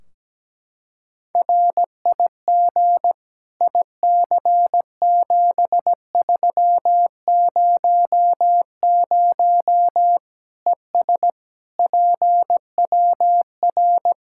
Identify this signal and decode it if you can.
Morse code, heard as a keyed tone, about 17 wpm, 700 Hz. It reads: RIG IC7300 ES PWR
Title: Practice QSO - Exchange 2 - Info - 17wpm